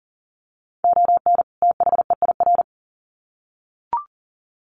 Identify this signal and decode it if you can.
Morse code, heard as a keyed tone, about 40 wpm, 700 Hz. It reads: ON THEIR
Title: Sets of 2 Words 40wpm